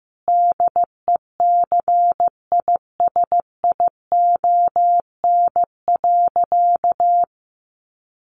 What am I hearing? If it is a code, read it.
Morse code, heard as a keyed tone, about 15 wpm, 700 Hz. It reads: DECISION.
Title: Sentences from Top 1200 Words 15wpm